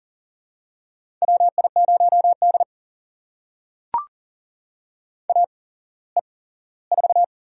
Morse code, heard as a keyed tone, about 44 wpm, spaced as for 20 wpm, 690 Hz. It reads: WI0D AE4
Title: USA Call Signs - 4 Characters - 8x Character Spacing - 40wpm